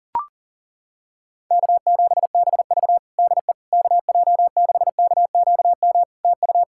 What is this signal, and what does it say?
Morse code, heard as a keyed tone, about 40 wpm, 700 Hz. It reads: K7BV DE KJ6KQM TU
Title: Practice QSO - Exchange 4 - Ending - 40wpm